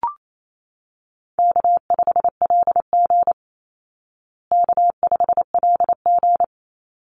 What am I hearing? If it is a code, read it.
Morse code, heard as a keyed tone, about 28 wpm, 700 Hz. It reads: K5LG K5LG
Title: Call Signs Repeated 3x - 28wpm